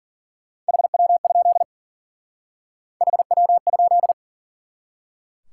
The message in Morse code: HW? HW?

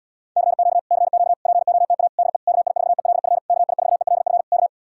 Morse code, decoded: CQ CQ POTA DE KI4ELF KI4ELF K

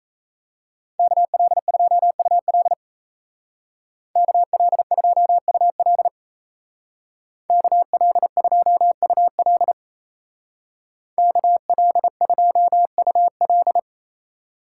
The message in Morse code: KL2UL KL2UL KL2UL KL2UL

